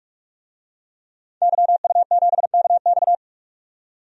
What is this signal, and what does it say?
Morse code, heard as a keyed tone, about 43 wpm, 700 Hz. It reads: YU7KX
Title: DXCC-100 Call Signs - 5 Characters - 45wpm